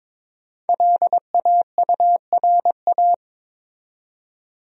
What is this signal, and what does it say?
Morse code, heard as a keyed tone, about 22 wpm, 700 Hz. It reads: LAURA
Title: USA Common Names - Max 5 Letters - 22wpm